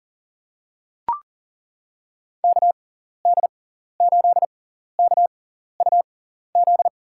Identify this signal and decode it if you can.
Morse code, heard as a keyed tone, about 40 wpm, spaced as for 14 wpm, 700 Hz. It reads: KD8KUZ
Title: USA Call Signs - 6 Characters - 6x Character Spacing - 40wpm